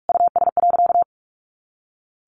Morse code, heard as a keyed tone, about 44 wpm, 700 Hz. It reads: US.
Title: A Sound Of Thunder 45wpm